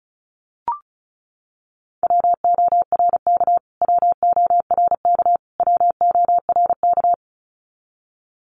Morse code, heard as a keyed tone, about 34 wpm, 700 Hz. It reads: WORK WORK WORK
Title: Top 200 Words Repeated 35wpm